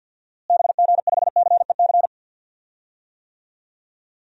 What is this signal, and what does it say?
Morse code, heard as a keyed tone, about 52 wpm, 700 Hz. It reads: BG5CEC
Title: DXCC-100 Call Signs - 6 Characters - Repeated 3x - 50wpm